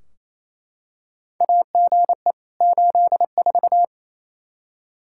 Morse code, heard as a keyed tone, about 28 wpm, 700 Hz. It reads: AGE 84
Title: Practice QSO - Exchange 3 - More Info - 28wpm